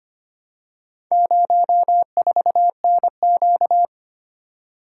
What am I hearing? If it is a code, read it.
Morse code, heard as a keyed tone, about 25 wpm, 700 Hz. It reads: 04NQ
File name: practice / 25wpm / Four Letter-Number Combinations - 25wpm